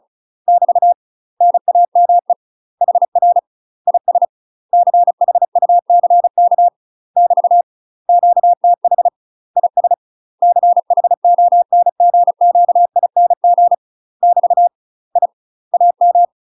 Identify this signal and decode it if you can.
Morse code, heard as a keyed tone, about 36 wpm, 700 Hz. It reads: <BT> NAME HR IS CHUCK <BT> QTH IS CHONGQING <BT> I AM
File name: UK MASTER QSOs TRAINING / 35WPM / UK MASTER QSOs 0038